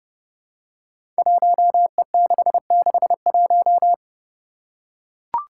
Morse code, heard as a keyed tone, about 30 wpm, 700 Hz. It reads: 1E661
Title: Five Challenging Letter-Number Combinations 30wpm